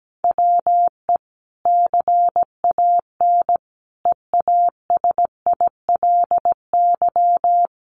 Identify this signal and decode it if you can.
Morse code, heard as a keyed tone, about 17 wpm, 700 Hz. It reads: WE CAN EASILY
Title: Sentences from Top 1200 Words 17wpm